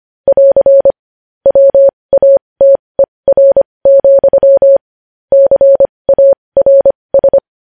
Morse code, heard as a keyed tone, about 25 wpm, spaced as for 20 wpm, 550 Hz. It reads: <AR> WATER, CARS
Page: news headlines in morse code at 20 wpm